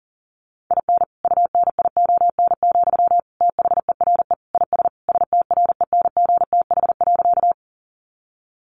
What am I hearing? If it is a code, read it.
Morse code, heard as a keyed tone, about 40 wpm, 700 Hz. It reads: IN UNION, THERE IS STRENGTH.
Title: Sentences from Top 1400 Words 40wpm